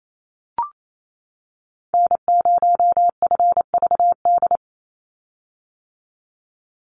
Morse code, heard as a keyed tone, about 27 wpm, 700 Hz. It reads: N0FVD
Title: USA Call Signs 28wpm